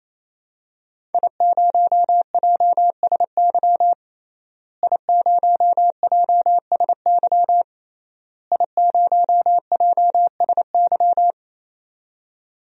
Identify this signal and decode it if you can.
Morse code, heard as a keyed tone, about 28 wpm, 700 Hz. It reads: I0JSY I0JSY I0JSY
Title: DXCC-100 Call Signs - 5 Characters - Repeated 3x - 28wpm